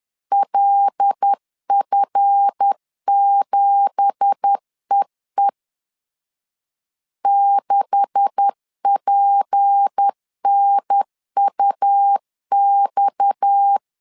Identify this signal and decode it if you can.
Morse code, heard as a keyed tone, about 10 wpm, 800 Hz. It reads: LF7EE 6PNUX
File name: CW / DET / 1 9 1500C S A 10wpm 4511